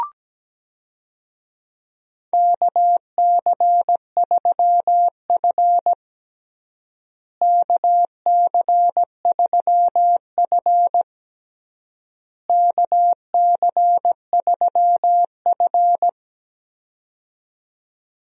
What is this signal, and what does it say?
Morse code, heard as a keyed tone, about 17 wpm, 700 Hz. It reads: KC3F KC3F KC3F
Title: USA Call Signs - 4 Characters - Repeated 3x - 17wpm